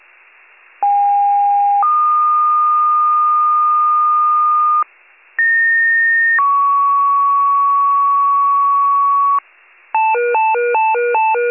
Two-tone page, heard as a one-second tone, then a three-second tone, about 810 then 1220 Hz, about 1850 then 1120 Hz. It is heard from 0.8 to 9.4 s.